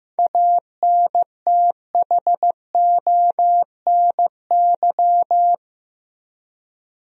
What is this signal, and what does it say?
Morse code, heard as a keyed tone, about 15 wpm, 700 Hz. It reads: ANTHONY
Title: USA Common Names - 15wpm